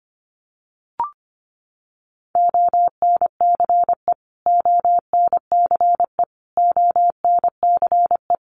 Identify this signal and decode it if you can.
Morse code, heard as a keyed tone, about 25 wpm, 700 Hz. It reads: ONCE ONCE ONCE
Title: Top 300 Words Repeated 25wpm